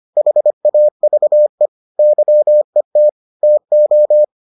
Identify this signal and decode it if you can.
Morse code, heard as a keyed tone, about 25 wpm, 600 Hz. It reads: HAVE YET TO